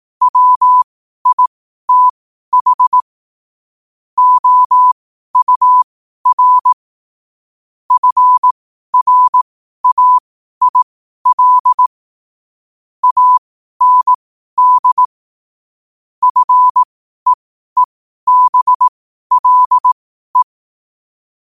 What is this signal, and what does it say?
Morse code, heard as a keyed tone, about 18 wpm, spaced as for 12 wpm, 1000 Hz. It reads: WITH OUR FRAIL AND FEEBLE
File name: qotdfiles / 12wpm